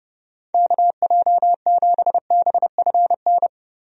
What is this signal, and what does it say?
Morse code, heard as a keyed tone, about 32 wpm, 700 Hz. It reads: KJ7BFN
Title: USA Call Signs - 6 Characters - Repeated 3x - 30wpm